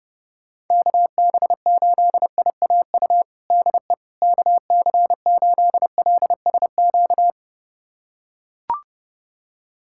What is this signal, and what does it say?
Morse code, heard as a keyed tone, about 30 wpm, 700 Hz. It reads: KB8IAU DE KC8LSQ
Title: USA Callsign DE Callsign - 6 Characters - 30wpm